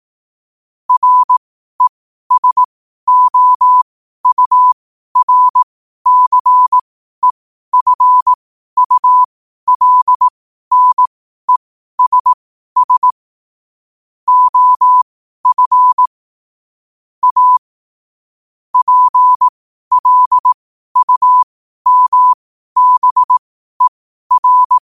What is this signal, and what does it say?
Morse code, heard as a keyed tone, about 18 wpm, spaced as for 12 wpm, 1000 Hz. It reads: RESOURCEFULNESS OF A PLUMBER
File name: qotdfiles / 12wpm